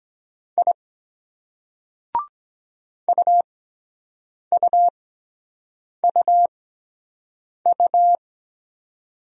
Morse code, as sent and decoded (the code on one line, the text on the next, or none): .. / ..- ..- ..- ..-
I UUUU